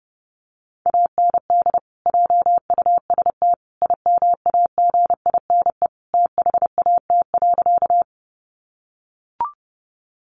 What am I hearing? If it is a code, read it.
Morse code, heard as a keyed tone, about 30 wpm, 700 Hz. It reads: AND JUST IMAGINE THAT.
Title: Sentences from Top 1400 Words 30wpm